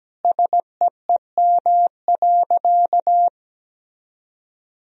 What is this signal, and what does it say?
Morse code, heard as a keyed tone, about 17 wpm, 700 Hz. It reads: SEEM.
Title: Sentences from Top 1000 Words - Review Part 1 - 17wpm